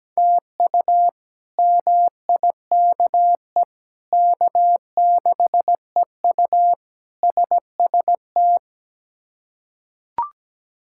Morse code, heard as a keyed tone, about 17 wpm, 700 Hz. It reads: TU MIKE K6EU SST
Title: K1USN SST QSO Practice - 17wpm